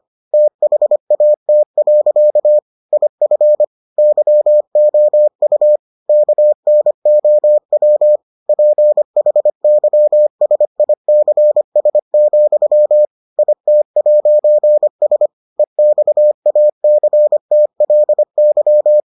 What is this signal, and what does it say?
Morse code, heard as a keyed tone, about 25 wpm, 600 Hz. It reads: THAT. IF YOU KNOW PHYSICS, IT'S EXACTLY